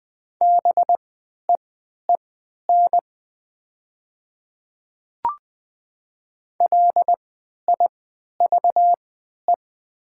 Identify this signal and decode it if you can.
Morse code, heard as a keyed tone, about 20 wpm, spaced as for 11 wpm, 700 Hz. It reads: BEEN LIVE
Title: 3x Character Spacing - Top 500 Words - 20wpm